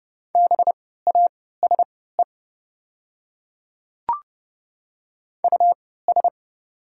Morse code, heard as a keyed tone, about 31 wpm, spaced as for 17 wpm, 690 Hz. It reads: BASE US
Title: 3x Character Spacing - Top 500 Words - 30wpm